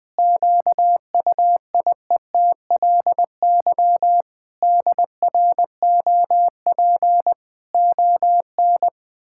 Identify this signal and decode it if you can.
Morse code, heard as a keyed tone, about 20 wpm, 700 Hz. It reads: QUIETLY DROP ON
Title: Pangram Challenge v2 - 20wpm